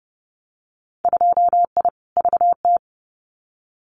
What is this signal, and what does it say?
Morse code, heard as a keyed tone, about 30 wpm, 700 Hz. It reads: 2I VT
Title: Winter Field Day Exchanges 30wpm